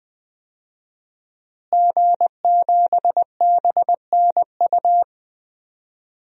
Morse code, heard as a keyed tone, about 20 wpm, 700 Hz. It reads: G7BNU
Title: DXCC-100 Call Signs - 5 Characters - Repeated 3x - 20wpm